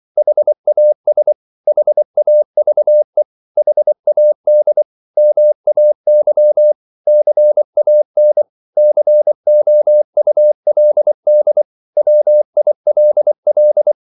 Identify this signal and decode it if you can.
Morse code, heard as a keyed tone, about 24 wpm, 600 Hz. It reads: HAS HAVE HAD MAY CAN COULD WILL
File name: CW / Practice / 500MostCommonEnglishWords24